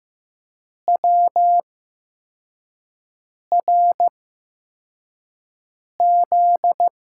Morse code, heard as a keyed tone, about 15 wpm, spaced as for 4 wpm, 700 Hz. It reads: WRZ
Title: USA Call Signs - 8x Character Spacing - 15wpm